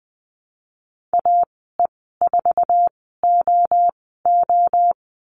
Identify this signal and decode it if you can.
Morse code, heard as a keyed tone, about 20 wpm, spaced as for 15 wpm, 700 Hz. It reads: AE4OO
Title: 069 Lesson 12g - Letter d - Call Signs 20wpm